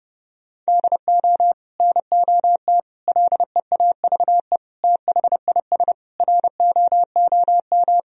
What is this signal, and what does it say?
Morse code, heard as a keyed tone, about 30 wpm, 700 Hz. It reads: DO NOT LEAVE THIS ROOM
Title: Sentences from Top 1000 Words - Review Part 1 - 30wpm